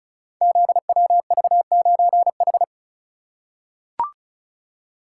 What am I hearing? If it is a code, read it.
Morse code, heard as a keyed tone, about 36 wpm, 700 Hz. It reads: ZWV9H
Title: Code Groups - 5 Letter:Numbers - 35wpm